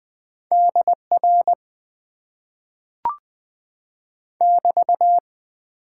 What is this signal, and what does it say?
Morse code, heard as a keyed tone, about 20 wpm, 700 Hz. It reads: DR <BT>
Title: Top 90 QSO Elements - 8x Character Spacing - 20wpm